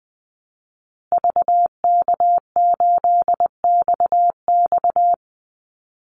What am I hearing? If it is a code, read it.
Morse code, heard as a keyed tone, about 20 wpm, 700 Hz. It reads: VK8XX
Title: Call Signs Repeated 2x - 20wpm